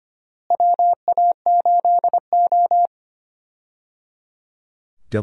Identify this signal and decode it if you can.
Morse code, heard as a keyed tone, about 25 wpm, 700 Hz. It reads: WA8O
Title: USA Call Signs - 4 Characters - Repeated 3x - 25wpm